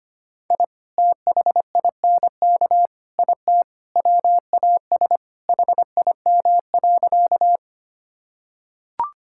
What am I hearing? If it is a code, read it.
Morse code, heard as a keyed tone, about 25 wpm, 700 Hz. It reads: I THINK IT WAS HIM.
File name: practice / 25wpm / Sentences with 4-Word Phrases - 25wpm